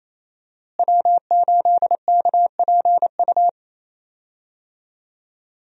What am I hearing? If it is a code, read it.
Morse code, heard as a keyed tone, about 27 wpm, 700 Hz. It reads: W8KPU
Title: USA Call Signs - 5 Characters - Repeated 2x - 28wpm